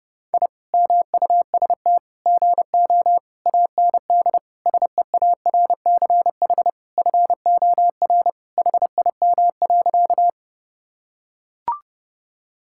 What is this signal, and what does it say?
Morse code, heard as a keyed tone, about 32 wpm, 700 Hz. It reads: I MUST GO AND SEARCH FOR HIM.
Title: Sentences from Top 1000 Words - Review Part 5 - 30wpm